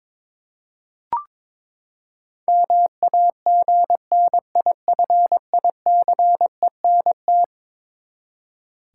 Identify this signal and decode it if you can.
Morse code, heard as a keyed tone, about 23 wpm, 700 Hz. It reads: MAGNIFICENT
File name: practice / 22wpm / Eleven Letter Words - 22wpm